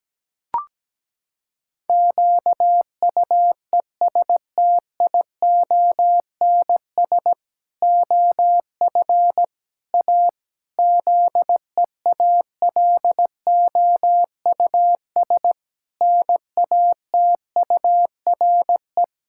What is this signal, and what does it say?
Morse code, heard as a keyed tone, about 17 wpm, 700 Hz. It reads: QUESTIONS OF A ZEALOUS NATURE
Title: Pangram Challenge v2 - 17wpm